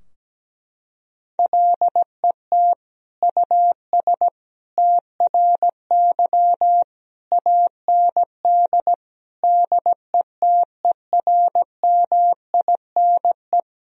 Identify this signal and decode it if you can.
Morse code, heard as a keyed tone, about 17 wpm, 700 Hz. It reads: LET US TRY AND DETERMINE
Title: Sentences from Top 1000 Words - Review Part 4 - 17wpm